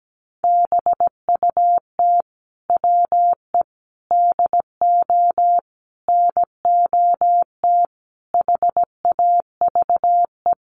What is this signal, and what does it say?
Morse code, heard as a keyed tone, about 17 wpm, 700 Hz. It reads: BUT WE DO NOT HAVE
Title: Sets of 5 Words Encore2 17wpm